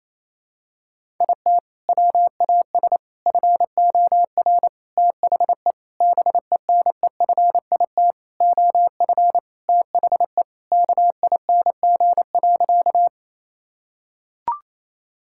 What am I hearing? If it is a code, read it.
Morse code, heard as a keyed tone, about 28 wpm, 700 Hz. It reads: IT WAS FOR THE BENEFIT OF THE KING.
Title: Sentences from Top 1000 Words - Review Part 6 - 28wpm